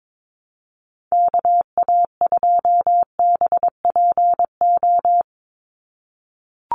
Morse code, heard as a keyed tone, about 22 wpm, 700 Hz. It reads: KA2BPO
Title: USA Call Signs 22wpm